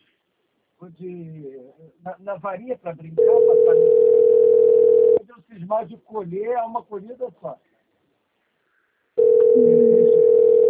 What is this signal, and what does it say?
Ringback tone (North America), 440 + 480 Hz, with speech.
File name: out- - - - - .61938